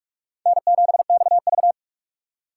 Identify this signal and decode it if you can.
Morse code, heard as a keyed tone, about 44 wpm, 700 Hz. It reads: N7XV